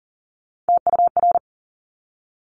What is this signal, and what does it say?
Morse code, heard as a keyed tone, about 40 wpm, 700 Hz. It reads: TUR
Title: Top Three Letter Combinations in Words 40wpm